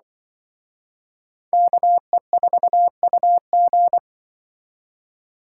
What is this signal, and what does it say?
Morse code, heard as a keyed tone, about 24 wpm, 700 Hz. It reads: KE4UG
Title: Speed Racing - USA 2x2 Call Signs 20wpm